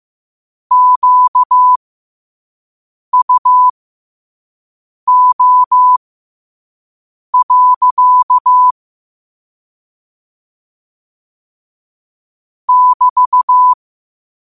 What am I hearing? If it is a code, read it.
Morse code, heard as a keyed tone, about 15 wpm, spaced as for 5 wpm, 1000 Hz. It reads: QUO. <BT>
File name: qotdfiles / 5wpm